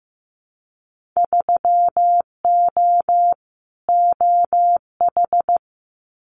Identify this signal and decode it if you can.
Morse code, heard as a keyed tone, about 15 wpm, 700 Hz. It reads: 3O OH
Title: Winter Field Day Exchanges 15wpm